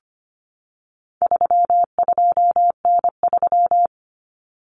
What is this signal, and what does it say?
Morse code, heard as a keyed tone, about 25 wpm, 700 Hz. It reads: 32N3
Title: Common Cut Numbers - 4-Digit 25wpm